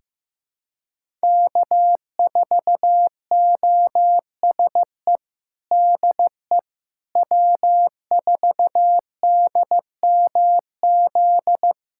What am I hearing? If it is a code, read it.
Morse code, heard as a keyed tone, about 16 wpm, 700 Hz. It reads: K4OSE DE W4DMZ
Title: USA Callsign DE Callsign - 5 Characters - 15wpm